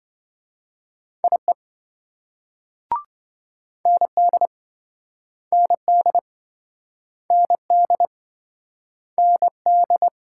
Morse code, heard as a keyed tone, about 28 wpm, 700 Hz. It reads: IE ND ND ND ND